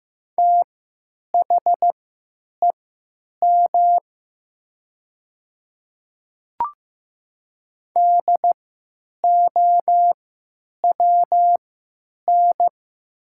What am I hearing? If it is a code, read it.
Morse code, heard as a keyed tone, about 15 wpm, spaced as for 9 wpm, 700 Hz. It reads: THEM DOWN
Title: 3x Character Spacing - Top 100 Words - 15wpm